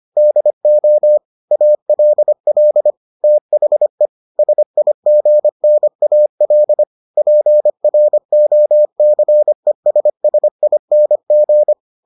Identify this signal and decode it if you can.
Morse code, heard as a keyed tone, about 25 wpm, 600 Hz. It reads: DO ALL THE SIGNAL PROCESSING